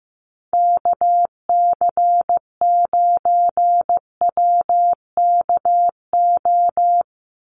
Morse code, heard as a keyed tone, about 15 wpm, 700 Hz. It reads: KC9WKO
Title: USA Call Signs 15wpm